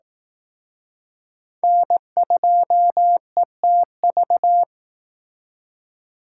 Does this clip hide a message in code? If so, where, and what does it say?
Yes: Morse code from 1.6 to 4.6 s, heard as a keyed tone, about 19 wpm, 700 Hz. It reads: N2ETV